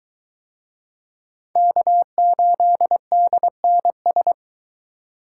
Morse code, heard as a keyed tone, about 22 wpm, 700 Hz. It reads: K8DNS